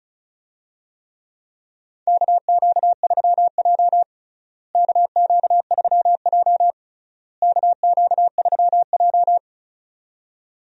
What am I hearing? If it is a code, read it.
Morse code, heard as a keyed tone, about 36 wpm, 700 Hz. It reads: KQ3J KQ3J KQ3J
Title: USA Call Signs - 4 Characters - Repeated 3x - 35wpm